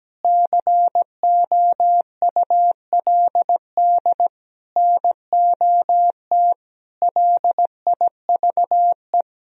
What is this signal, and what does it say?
Morse code, heard as a keyed tone, about 17 wpm, 700 Hz. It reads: COULD NOT LIVE
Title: Sentences from Top 1000 Words - Review Part 3 - 17wpm